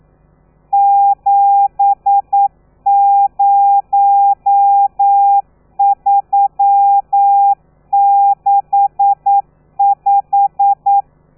Morse code, heard as a keyed tone, about 9 wpm, 800 Hz. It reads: 70365
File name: telegrafia / i2tfj / Lez13 40